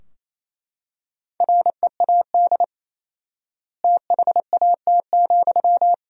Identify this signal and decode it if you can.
Morse code, heard as a keyed tone, about 28 wpm, 700 Hz. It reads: READ THAT,
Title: 4x Word-Spacing - Sentences from Top 200 Words - 28wpm